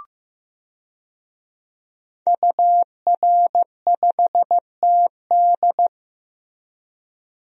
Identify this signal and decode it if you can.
Morse code, heard as a keyed tone, about 15 wpm, 700 Hz. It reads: UR5TD
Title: DXCC-100 Call Signs - 5 Characters - 15wpm